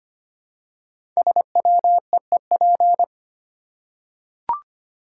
Morse code, heard as a keyed tone, about 25 wpm, 700 Hz. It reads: SWEEP
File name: 5 Letter Words - First 3 Letters Given - 25wpm